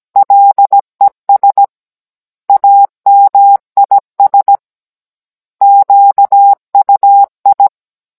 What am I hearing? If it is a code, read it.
Morse code, heard as a keyed tone, about 17 wpm, 800 Hz. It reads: LES AMIS QUI
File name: Cours-NR19-16mm